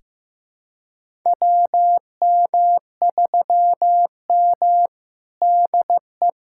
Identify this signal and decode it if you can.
Morse code, heard as a keyed tone, about 15 wpm, 700 Hz. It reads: WM3M DE
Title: USA Callsign DE Callsign - 4 Characters - 15wpm